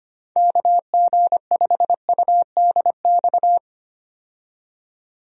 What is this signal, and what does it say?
Morse code, heard as a keyed tone, about 25 wpm, 700 Hz. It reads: KG5UDX
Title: USA Call Signs - 6 Characters - 25wpm